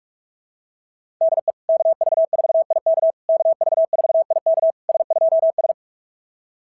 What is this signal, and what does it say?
Morse code, heard as a keyed tone, about 44 wpm, 660 Hz. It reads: DE KV4IK KV4IK S2S
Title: SOTA QSO Practice 45wpm